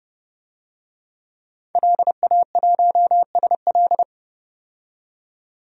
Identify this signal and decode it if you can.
Morse code, heard as a keyed tone, about 30 wpm, 700 Hz. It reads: LA1SL
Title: DXCC-100 Call Signs - 5 Characters - 30wpm